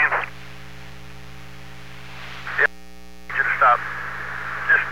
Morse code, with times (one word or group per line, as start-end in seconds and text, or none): none